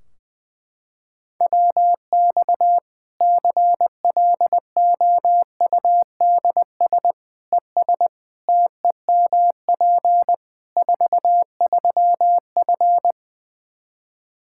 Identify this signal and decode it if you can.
Morse code, heard as a keyed tone, about 20 wpm, 700 Hz. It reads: WX CLOUDS ES TEMP 43F
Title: Practice QSO - Exchange 2 - Info - 20wpm